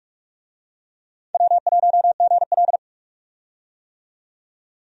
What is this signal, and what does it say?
Morse code, heard as a keyed tone, about 46 wpm, 700 Hz. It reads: W1GL
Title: USA Call Signs - 4 Characters - 45wpm